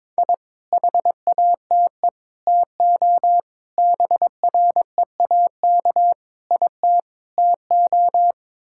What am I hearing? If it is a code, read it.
Morse code, heard as a keyed tone, about 22 wpm, 700 Hz. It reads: I HATE TO BREAK IT TO
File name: practice / 22wpm / Sets of 8 Words - 22wpm